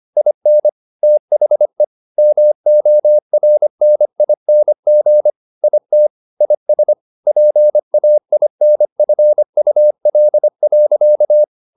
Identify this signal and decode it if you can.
Morse code, heard as a keyed tone, about 25 wpm, 600 Hz. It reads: IN THE MORNING IT IS PAINFUL.
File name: the old man and the sea part 1 25wpm